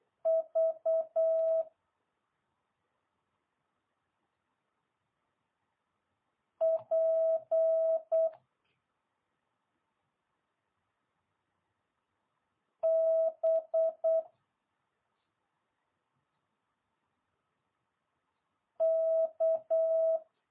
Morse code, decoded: V P B K

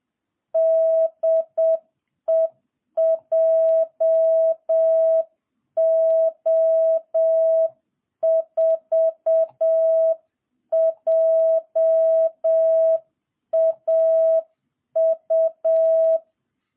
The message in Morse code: DE JO4JAU

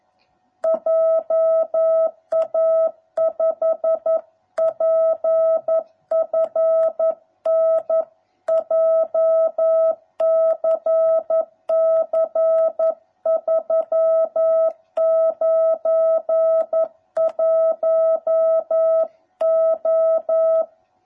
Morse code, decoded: JA5PFN JCC 391O

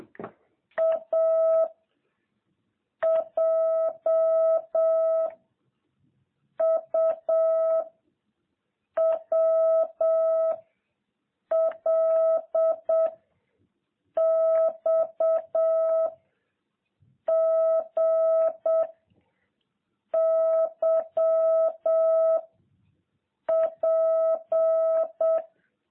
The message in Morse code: A J U W L X G Y P

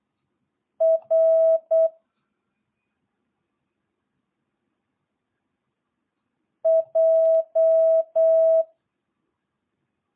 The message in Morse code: R J

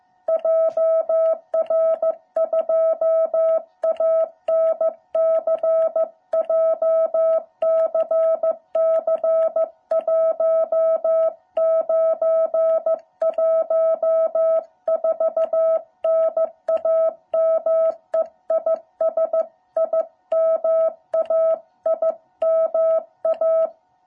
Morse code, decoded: JR2ANC JCC 1914 NAME IS IMA IMA

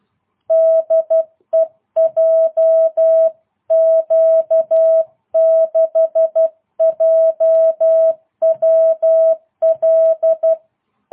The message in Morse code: DE JQ6JWL

